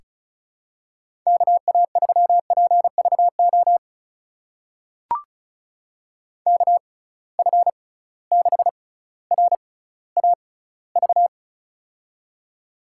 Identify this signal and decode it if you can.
Morse code, heard as a keyed tone, about 36 wpm, spaced as for 18 wpm, 700 Hz. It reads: KA3PVO KF6RAV
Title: USA Call Signs - 6 Characters - 6x Character Spacing - 35wpm